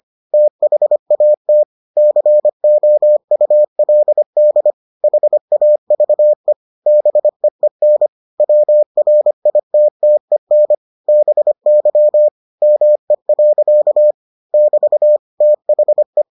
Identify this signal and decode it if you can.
Morse code, heard as a keyed tone, about 25 wpm, 600 Hz. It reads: THAT COULD HAVE BEEN WRITTEN BY ME. <BT> THE